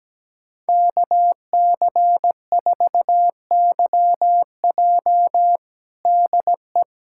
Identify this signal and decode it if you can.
Morse code, heard as a keyed tone, about 17 wpm, 700 Hz. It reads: KC4YJ DE
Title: USA Callsign DE Callsign - 5 Characters - 17wpm